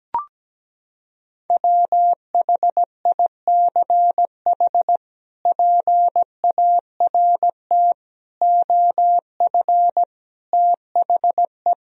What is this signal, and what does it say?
Morse code, heard as a keyed tone, about 17 wpm, 700 Hz. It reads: WHICH PART OF THE
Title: Sentences from Top 1000 Words - Review Part 2 - 17wpm